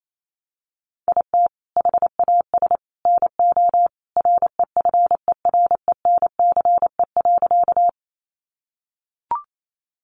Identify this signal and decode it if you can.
Morse code, heard as a keyed tone, about 28 wpm, 700 Hz. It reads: IT HAS NO REFERENCE.
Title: Sentences from Top 1200 Words 28wpm